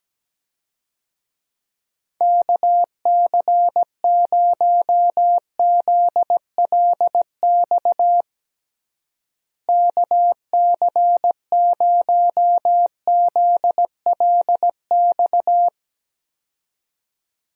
Morse code, heard as a keyed tone, about 17 wpm, 700 Hz. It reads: KC0ZLX KC0ZLX
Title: USA Call Signs - 6 Characters - Repeated 2x - 17wpm